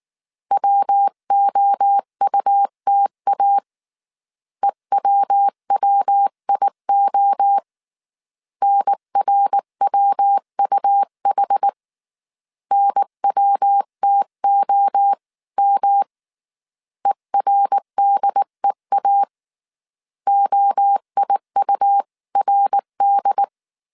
Morse code, heard as a keyed tone, about 19 wpm, 800 Hz. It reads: WOUTA EWWIO NRWUH NWTOM ERDEA OIURD